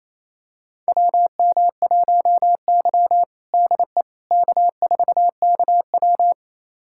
Morse code, heard as a keyed tone, about 28 wpm, 700 Hz. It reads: WM1Y DE K4KW
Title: USA Callsign DE Callsign - 4 Characters - 28wpm